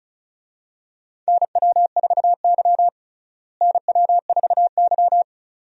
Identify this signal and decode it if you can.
Morse code, heard as a keyed tone, about 36 wpm, 700 Hz. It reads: NW4Y NW4Y
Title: USA Call Signs - 4 Characters - Repeated 2x - 35wpm